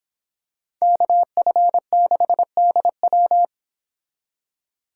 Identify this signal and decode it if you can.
Morse code, heard as a keyed tone, about 25 wpm, 700 Hz. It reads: KF6DW